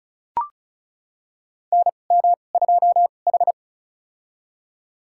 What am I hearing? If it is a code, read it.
Morse code, heard as a keyed tone, about 34 wpm, spaced as for 25 wpm, 700 Hz. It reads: NM2H
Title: USA Call Signs - 4 Characters - 2x Character Spacing - 35wpm